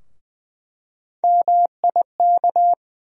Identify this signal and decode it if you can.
Morse code, heard as a keyed tone, about 20 wpm, 700 Hz. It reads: MIK
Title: USA Call Sign Postfixes - 3 Letters 20wpm